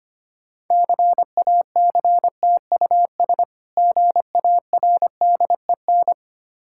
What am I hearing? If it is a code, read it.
Morse code, heard as a keyed tone, about 25 wpm, 690 Hz. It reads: CACTUS GARDEN
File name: practice / 25wpm / Difficult Words with Hints - 25wpm